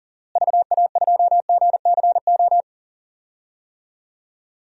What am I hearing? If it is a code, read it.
Morse code, heard as a keyed tone, about 37 wpm, 700 Hz. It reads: VA2GCO
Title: DXCC-100 Call Signs - 6 Characters - Repeated 2x - 40wpm